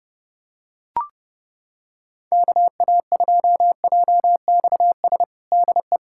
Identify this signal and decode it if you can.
Morse code, heard as a keyed tone, about 32 wpm, 700 Hz. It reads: KA2JXS DE
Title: USA Callsign DE Callsign - 6 Characters - 30wpm